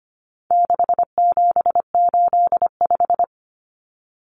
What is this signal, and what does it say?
Morse code, heard as a keyed tone, about 25 wpm, 700 Hz. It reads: 6785
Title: Common Cut Numbers - 4-Digit 25wpm